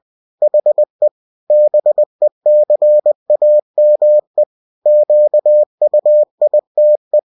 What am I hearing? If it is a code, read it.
Morse code, heard as a keyed tone, about 20 wpm, 600 Hz. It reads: HE BECAME QUITE
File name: cw-books / winnie-the-pooh / winnie-20wpm-page-0057